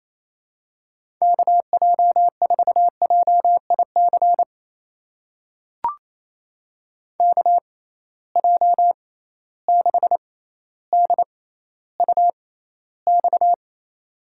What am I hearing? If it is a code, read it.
Morse code, heard as a keyed tone, about 27 wpm, spaced as for 14 wpm, 700 Hz. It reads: KJ4JIC KJ6DUX
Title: USA Call Signs - 6 Characters - 6x Character Spacing - 28wpm